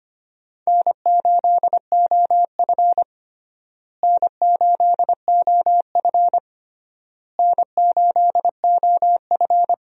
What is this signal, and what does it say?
Morse code, heard as a keyed tone, about 25 wpm, 700 Hz. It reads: N8OF N8OF N8OF
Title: USA Call Signs - 4 Characters - Repeated 3x - 25wpm